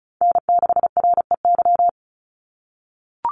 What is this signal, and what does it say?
Morse code, heard as a keyed tone, about 34 wpm, 700 Hz. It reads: N6REY
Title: USA Call Signs 35wpm